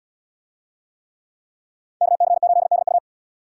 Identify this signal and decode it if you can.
Morse code, heard as a keyed tone, about 77 wpm, 700 Hz. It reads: KC8DU